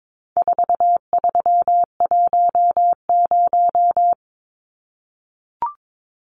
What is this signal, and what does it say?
Morse code, heard as a keyed tone, about 22 wpm, 700 Hz. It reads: 4310
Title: Numbers - 4-Digit 22wpm